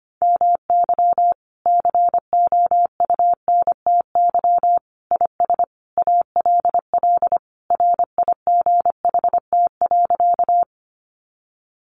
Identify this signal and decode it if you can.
Morse code, heard as a keyed tone, about 25 wpm, 700 Hz. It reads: MY COUNTY IS ALL RIGHT.
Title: Sentences from Top 1200 Words 25wpm